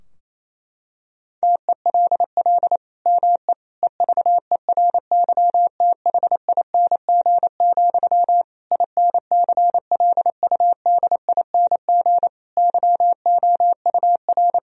Morse code, heard as a keyed tone, about 28 wpm, 700 Hz. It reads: TELL ME EVERYTHING, INCLUDING YOUR
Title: Sentences from Top 1000 Words - Review Part 4 - 28wpm